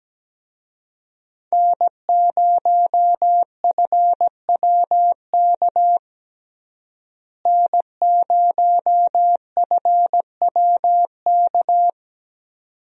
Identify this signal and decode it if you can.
Morse code, heard as a keyed tone, about 17 wpm, 700 Hz. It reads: N0FWK N0FWK
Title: USA Call Signs - 5 Characters - Repeated 2x - 17wpm